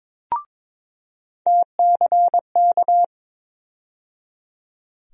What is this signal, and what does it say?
Morse code, heard as a keyed tone, about 22 wpm, 700 Hz. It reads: TCK
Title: USA Call Sign Postfixes - 3 Letters 22wpm